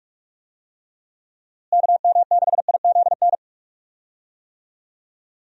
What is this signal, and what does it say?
Morse code, heard as a keyed tone, about 46 wpm, 700 Hz. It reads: KM6IZN
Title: USA Call Signs - 6 Characters - Repeated 3x - 45wpm